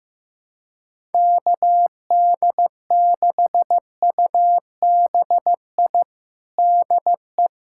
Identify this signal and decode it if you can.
Morse code, heard as a keyed tone, about 15 wpm, 700 Hz. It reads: KD6UBI DE
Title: USA Callsign DE Callsign - 6 Characters - 15wpm